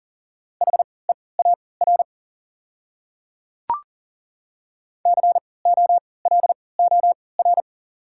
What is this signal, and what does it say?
Morse code, heard as a keyed tone, about 40 wpm, spaced as for 23 wpm, 700 Hz. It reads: HEAR COLOR
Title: 3x Character Spacing - Top 300 Words - 40wpm